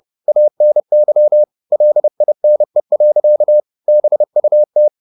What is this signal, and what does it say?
Morse code, heard as a keyed tone, about 30 wpm, 600 Hz. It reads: ANY LINE. BUT